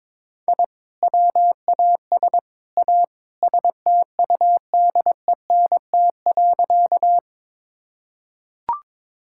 Morse code, heard as a keyed tone, about 22 wpm, 700 Hz. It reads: I WAS A STUDENT.
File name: practice / 22wpm / Sentences with 3-Word Phrases - 22wpm